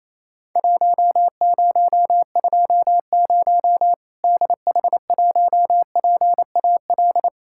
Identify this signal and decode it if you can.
Morse code, heard as a keyed tone, about 28 wpm, 700 Hz. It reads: 1020 DH1PAL
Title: Worked All Europe DX Contest - QTC Exchange - 28wpm